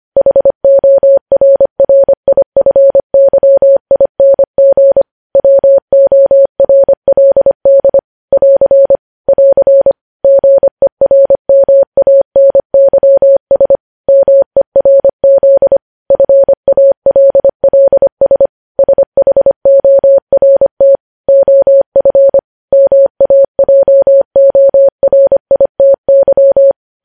Morse code, heard as a keyed tone, about 25 wpm, 550 Hz. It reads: HORRIFYING WORLD <AR> <AR> GERMANYS MERZ FALLS SHORT OF MAJORITY
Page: news-headlines-in-morse-code-at-25-wpm